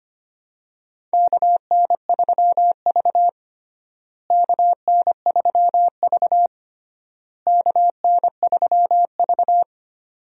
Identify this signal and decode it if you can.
Morse code, heard as a keyed tone, about 25 wpm, 700 Hz. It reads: KN3V KN3V KN3V
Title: USA Call Signs - 4 Characters - Repeated 3x - 25wpm